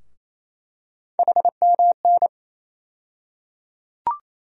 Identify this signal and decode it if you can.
Morse code, heard as a keyed tone, about 28 wpm, 700 Hz. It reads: HMN
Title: USA Call Sign Postfixes - 3 Letters 28wpm